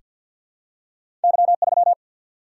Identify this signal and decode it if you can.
Morse code, heard as a keyed tone, about 48 wpm, 700 Hz. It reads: Y 3
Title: Single-Letter Single-Number 50wpm